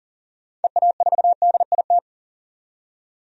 Morse code, heard as a keyed tone, about 40 wpm, 700 Hz. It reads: EA4DIT
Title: DXCC-100 Call Signs - 6 Characters - 40wpm